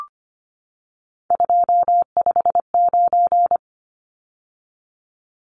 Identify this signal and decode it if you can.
Morse code, heard as a keyed tone, about 25 wpm, 700 Hz. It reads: 259
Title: Numbers - 3-Digit 25wpm